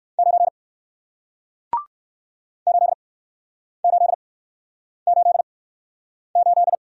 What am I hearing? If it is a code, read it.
Morse code, heard as a keyed tone, about 61 wpm, spaced as for 9 wpm, 700 Hz. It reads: 9 8888